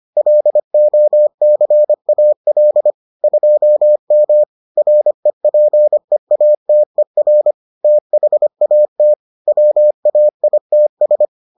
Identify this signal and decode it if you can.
Morse code, heard as a keyed tone, about 25 wpm, 600 Hz. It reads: LOCAL 2M REPEATER THAT WAITS